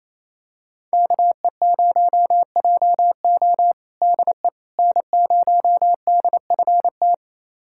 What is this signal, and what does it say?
Morse code, heard as a keyed tone, about 28 wpm, 700 Hz. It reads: KE0JO DE N0DFT
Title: USA Callsign DE Callsign - 5 Characters - 28wpm